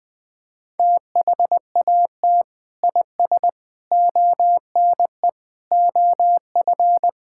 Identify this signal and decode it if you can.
Morse code, heard as a keyed tone, about 19 wpm, 700 Hz. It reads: THAT IS ONE OF
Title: Sentences with 5-Word Phrases - Spoken Beginning - 20wpm